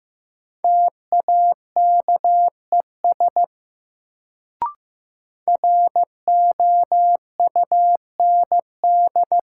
Morse code, heard as a keyed tone, about 15 wpm, 700 Hz. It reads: TAKES ROUND
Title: Sets of 5 Words Encore - New Word Drill - 15wpm